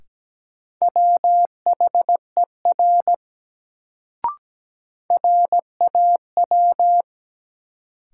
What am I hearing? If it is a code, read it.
Morse code, heard as a keyed tone, about 17 wpm, 700 Hz. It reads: WHER RAW
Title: Sentences from Top 1500 Words - New Word Drill - 17wpm